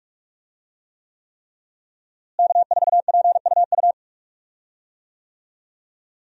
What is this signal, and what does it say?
Morse code, heard as a keyed tone, about 46 wpm, 700 Hz. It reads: K4PUU
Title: USA Call Signs - 5 Characters - Repeated 3x - 45wpm